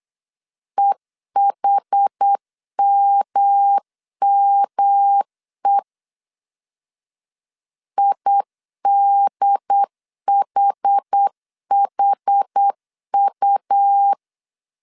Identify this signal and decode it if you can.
Morse code, heard as a keyed tone, about 8 wpm, 800 Hz. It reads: EHMME IDHHU